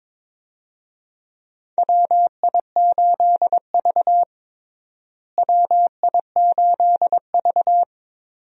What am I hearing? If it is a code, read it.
Morse code, heard as a keyed tone, about 22 wpm, 700 Hz. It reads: WI8V WI8V
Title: USA Call Signs - 4 Characters - Repeated 3x - 22wpm